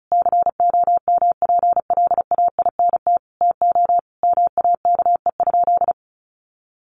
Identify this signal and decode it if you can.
Morse code, heard as a keyed tone, about 34 wpm, 700 Hz. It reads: COMPLAINT TO MAKE?
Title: Sentences from Top 1400 Words 35wpm